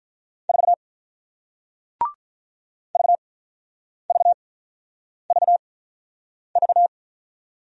Morse code, heard as a keyed tone, about 47 wpm, spaced as for 9 wpm, 700 Hz. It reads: 4 VVVV